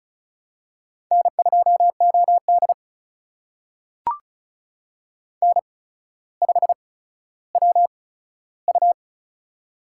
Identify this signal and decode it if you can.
Morse code, heard as a keyed tone, about 36 wpm, spaced as for 15 wpm, 700 Hz. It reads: N2OD N5WU
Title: USA Call Signs - 4 Characters - 8x Character Spacing - 35wpm